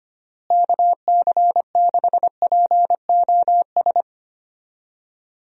Morse code, heard as a keyed tone, about 25 wpm, 700 Hz. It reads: KC6POS
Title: USA Call Signs - 6 Characters - Repeated 2x - 25wpm